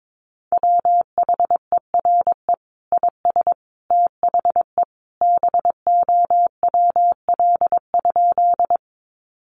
Morse code, heard as a keyed tone, about 22 wpm, 700 Hz. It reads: WHERE IS THE BOWL?
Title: Sentences from Top 1200 Words 22wpm